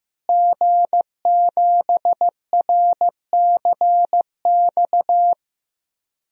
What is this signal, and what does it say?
Morse code, heard as a keyed tone, about 15 wpm, 700 Hz. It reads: G7RCX
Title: DXCC-100 Call Signs - 5 Characters - Repeated 3x - 15wpm